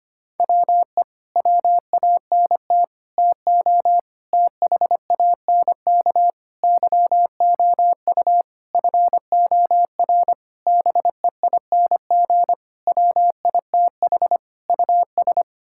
Morse code, heard as a keyed tone, about 25 wpm, 700 Hz. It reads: WE WANT TO THANK YOU FOR BEING WITH US
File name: Sets of 9 Words - 25wpm